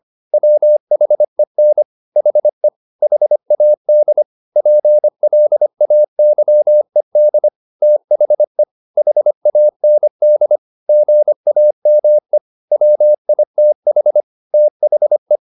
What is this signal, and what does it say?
Morse code, heard as a keyed tone, about 25 wpm, 600 Hz. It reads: WHEN HE HAD PLAYED THE HAND GAME WITH THE
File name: the old man and the sea part 1 25wpm